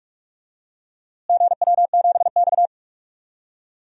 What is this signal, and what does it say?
Morse code, heard as a keyed tone, about 41 wpm, 700 Hz. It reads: GW7X
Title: DXCC-100 Call Signs - 4 Characters - Repeated 3x - 45wpm